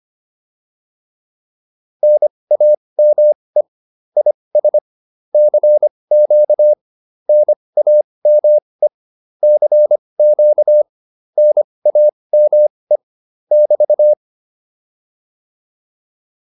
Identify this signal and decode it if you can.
Morse code, heard as a keyed tone, about 25 wpm, spaced as for 20 wpm, 600 Hz. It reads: NAME IS CQ NAME CQ NAME <BT>